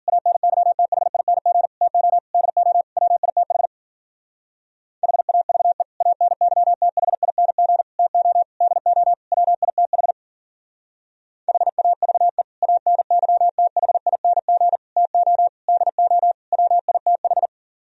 Morse code, heard as a keyed tone, about 48 wpm, 700 Hz. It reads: ANYTHING TO DO WITH HAVE ANYTHING TO DO WITH HAVE ANYTHING TO DO WITH
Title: Speed-Racing - Sets of 5 Words 40wpm